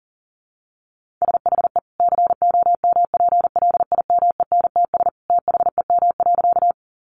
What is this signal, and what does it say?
Morse code, heard as a keyed tone, about 40 wpm, 700 Hz. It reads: SHE COMPLIMENTS THEM.
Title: Easy Phrases 40wpm